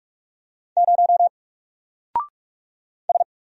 Morse code, heard as a keyed tone, about 44 wpm, 700 Hz. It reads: 0 S
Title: Single Character (Letter:Number:Punctuation:Prosign) 45wpm